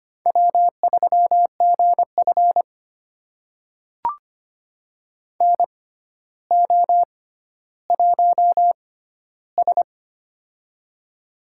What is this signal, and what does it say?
Morse code, heard as a keyed tone, about 25 wpm, spaced as for 13 wpm, 700 Hz. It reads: W3GF NO1S
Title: USA Call Signs - 6x Character Spacing - 25wpm